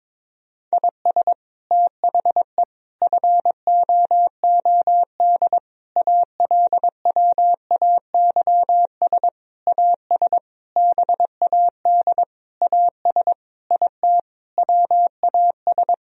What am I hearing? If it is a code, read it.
Morse code, heard as a keyed tone, about 22 wpm, 700 Hz. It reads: IS THE FOOD ALWAYS AS BAD AS IT WAS
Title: Sentences from Top 1000 Words - Review Part 1 - 22wpm